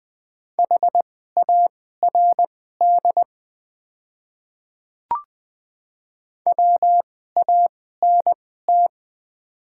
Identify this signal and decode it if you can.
Morse code, heard as a keyed tone, about 20 wpm, spaced as for 15 wpm, 700 Hz. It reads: HARD WANT
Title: 2x Character Spacing - Top 300 Words - 20wpm